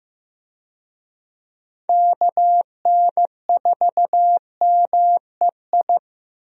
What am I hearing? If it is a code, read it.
Morse code, heard as a keyed tone, about 15 wpm, 700 Hz. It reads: KN4MEI
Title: USA Call Signs - 6 Characters - 15wpm